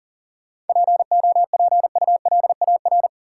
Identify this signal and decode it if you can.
Morse code, heard as a keyed tone, about 40 wpm, 700 Hz. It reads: POPULAR
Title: Sentences from Top 1000 Words - Review Part 3 - New Word Drill - 40wpm